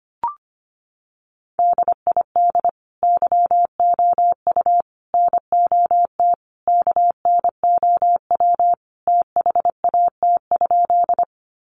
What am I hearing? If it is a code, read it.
Morse code, heard as a keyed tone, about 25 wpm, 700 Hz. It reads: DID YOU NOT KNOW THAT?
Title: Sentences from Top 600 Words 25wpm